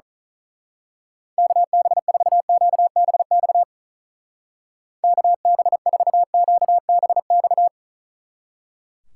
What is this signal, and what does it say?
Morse code, heard as a keyed tone, about 37 wpm, 700 Hz. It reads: KB4QBX KB4QBX
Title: Speed Racing - USA 2x3 Call Signs 35wpm